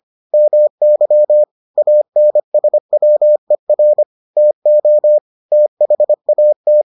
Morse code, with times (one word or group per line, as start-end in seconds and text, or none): 0.3-1.4 MY
1.8-4.0 ANSWER
4.4-5.2 TO
5.5-6.8 THAT